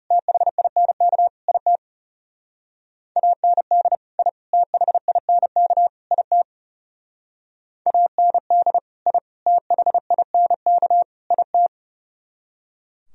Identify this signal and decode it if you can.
Morse code, heard as a keyed tone, about 37 wpm, 700 Hz. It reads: THINK IT AND I THINK IT AND I THINK IT
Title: Speed-Racing - Sets of 4 Words 30wpm